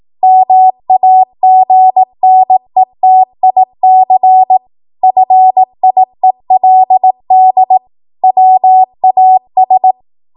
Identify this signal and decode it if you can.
Morse code, heard as a keyed tone, about 18 wpm, 750 Hz. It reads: MAGNETIC FIELD WAS
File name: Archive / 18 WPM / 18WPM